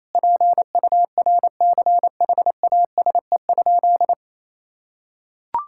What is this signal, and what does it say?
Morse code, heard as a keyed tone, about 28 wpm, 700 Hz. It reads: PURCHASE?
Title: Sentences from Top 1000 Words - Review Part 5 - 28wpm